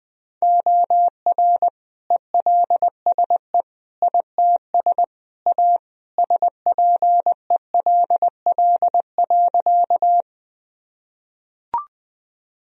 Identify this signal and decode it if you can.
Morse code, heard as a keyed tone, about 20 wpm, 700 Hz. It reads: OR ELSE ITS A SPELL.
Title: Sentences from Top 1000 Words - Review Part 1 - 20wpm